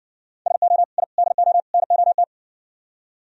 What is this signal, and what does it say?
Morse code, heard as a keyed tone, about 59 wpm, 700 Hz. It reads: SO I DO NOT